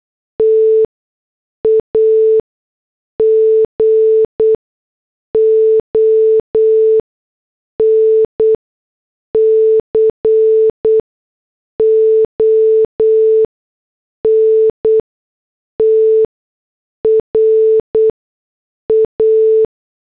Morse code, heard as a keyed tone, about 8 wpm, 440 Hz. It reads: T A G O N C O N T R A
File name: morse / headlines